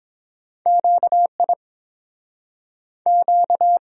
Morse code, heard as a keyed tone, about 30 wpm, 700 Hz. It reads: Q I Q